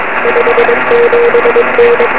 Morse code, heard as a keyed tone, about 22 wpm, 470 Hz, with radio noise. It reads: 57N
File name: 2111z pa2m 23cm